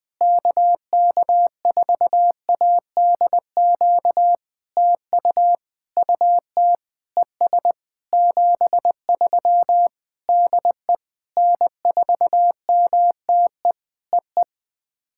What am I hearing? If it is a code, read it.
Morse code, heard as a keyed tone, about 18 wpm, 700 Hz. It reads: KK4ADQ TU UT ES 73 DE N4MTE EE
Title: POTA QSO Practice - 20wpm